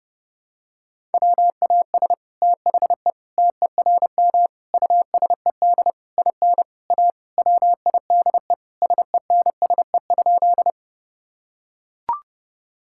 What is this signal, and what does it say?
Morse code, heard as a keyed tone, about 32 wpm, 700 Hz. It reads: WAS THE TERM USED IN A WIDE SENSE?
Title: Sentences from Top 1000 Words - Review Part 6 - 30wpm